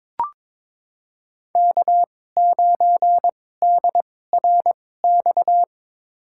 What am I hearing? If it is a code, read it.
Morse code, heard as a keyed tone, about 22 wpm, spaced as for 16 wpm, 700 Hz. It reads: K9DRX
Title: USA Call Signs - 5 Characters - 2x Character Spacing - 22wpm